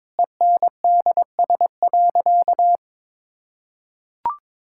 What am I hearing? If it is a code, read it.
Morse code, heard as a keyed tone, about 21 wpm, 700 Hz. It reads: ENDS.
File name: Difficult Sentences with Hints - 22wpm